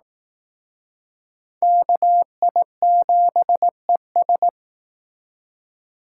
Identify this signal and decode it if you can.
Morse code, heard as a keyed tone, about 18 wpm, 700 Hz. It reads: KI7ES
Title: Speed Racing - USA 2x2 Call Signs 15wpm